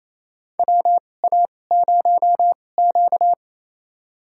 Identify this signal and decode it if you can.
Morse code, heard as a keyed tone, about 28 wpm, spaced as for 20 wpm, 700 Hz. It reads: WA0Q
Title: USA Call Signs - 4 Characters - 2x Character Spacing - 28wpm